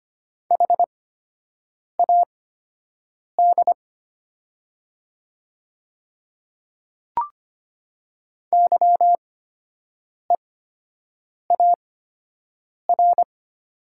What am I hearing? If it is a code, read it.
Morse code, heard as a keyed tone, about 25 wpm, spaced as for 7 wpm, 700 Hz. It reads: HAD YEAR
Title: 8x Character Spacing - Top 300 Words - 25wpm